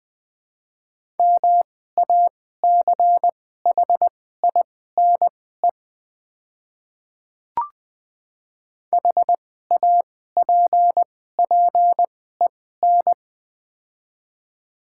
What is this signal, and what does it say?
Morse code, heard as a keyed tone, about 20 wpm, spaced as for 14 wpm, 700 Hz. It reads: MACHINE HAPPEN
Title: 2x Character Spacing - Top 500 Words - 20wpm